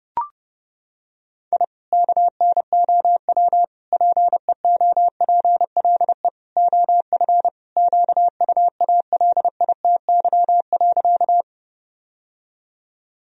Morse code, heard as a keyed tone, about 32 wpm, 700 Hz. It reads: I KNOW PEOPLE OF QUALITY.
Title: Sentences from Top 1000 Words - Review Part 4 - 30wpm